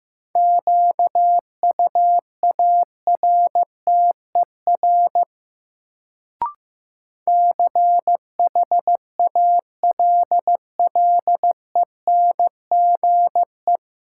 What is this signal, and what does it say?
Morse code, heard as a keyed tone, about 15 wpm, 700 Hz. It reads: QUARTER CHALLENGE
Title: Sentences from Top 1300 Words - New Word Drill Alt - 15wpm